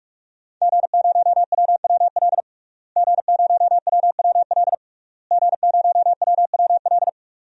Morse code, heard as a keyed tone, about 41 wpm, 700 Hz. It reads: G0WWL G0WWL G0WWL
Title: DXCC-100 Call Signs - 5 Characters - Repeated 3x - 45wpm